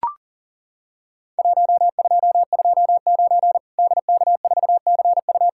Morse code, heard as a keyed tone, about 40 wpm, 700 Hz. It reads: 1229 DK4CU
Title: Worked All Europe DX Contest - QTC Exchange - 40wpm